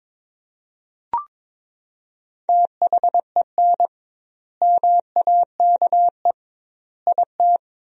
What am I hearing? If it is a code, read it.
Morse code, heard as a keyed tone, about 22 wpm, 700 Hz. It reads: THEN MAKE IT
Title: 2x Word-Spacing - Sentences from Top 100 Words - 22wpm